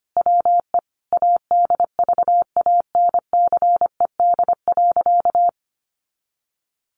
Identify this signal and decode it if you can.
Morse code, heard as a keyed tone, about 25 wpm, 700 Hz. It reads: WE ADVANCED.
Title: Sentences from Top 1200 Words 25wpm